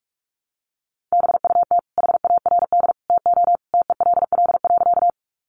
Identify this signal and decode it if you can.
Morse code, heard as a keyed tone, about 44 wpm, 700 Hz. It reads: BUT HARD TO TELL.
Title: Sentences from Top 300 Words 45wpm